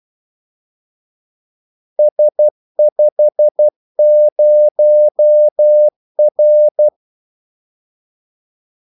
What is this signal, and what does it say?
Morse code, heard as a keyed tone, about 12 wpm, 600 Hz. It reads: S50R
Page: Kaldesignaler 12 wpm